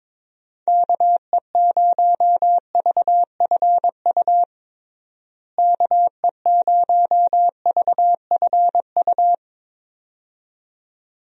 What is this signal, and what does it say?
Morse code, heard as a keyed tone, about 22 wpm, 700 Hz. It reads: KE0VFU KE0VFU
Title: USA Call Signs - 6 Characters - Repeated 2x - 22wpm